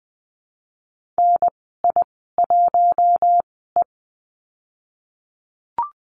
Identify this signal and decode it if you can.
Morse code, heard as a keyed tone, about 20 wpm, spaced as for 15 wpm, 700 Hz. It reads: NI1E
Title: 076 Lesson 13g - Letter L - Call Signs 20wpm